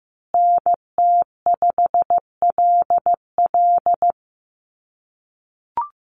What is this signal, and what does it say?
Morse code, heard as a keyed tone, about 15 wpm, 700 Hz. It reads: NT5LL
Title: USA Call Signs Repeated 3x 15wpm